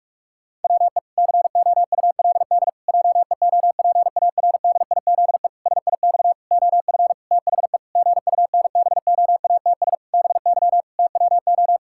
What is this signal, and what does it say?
Morse code, heard as a keyed tone, about 48 wpm, 700 Hz. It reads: WE COULD JEOPARDIZE SIX OF THE GUNBOATS BY TWO
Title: Pangram Challenge v2 - 45wpm